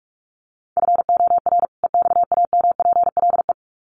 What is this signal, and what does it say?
Morse code, heard as a keyed tone, about 44 wpm, 700 Hz. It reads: FOR EXAMPLE
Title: Sets of 2 Words 45wpm